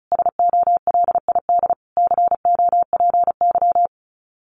Morse code, heard as a keyed tone, about 34 wpm, 700 Hz. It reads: SOLID COPY
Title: Common QSO Phrases Encore 35wpm